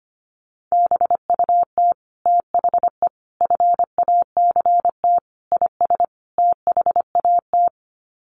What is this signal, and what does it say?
Morse code, heard as a keyed tone, about 25 wpm, 700 Hz. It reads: BUT THE FACT IS THAT
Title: Sets of 5 Words Encore2 25wpm